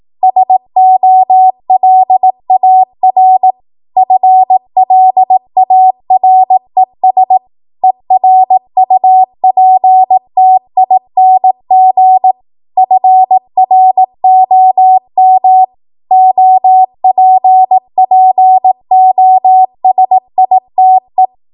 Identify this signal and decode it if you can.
Morse code, heard as a keyed tone, about 18 wpm, 750 Hz. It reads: SOLAR FLARES ERUPTING FROM OPPOSITE
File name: Morse / 18WPM